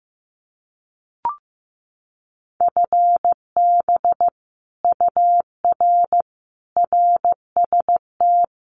Morse code, heard as a keyed tone, about 15 wpm, 700 Hz. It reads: FB UR RST
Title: Common QSO Phrases Encore 15wpm